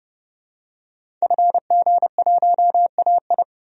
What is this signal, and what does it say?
Morse code, heard as a keyed tone, about 32 wpm, 700 Hz. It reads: FG1AI
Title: DXCC-100 Call Signs - 5 Characters - 30wpm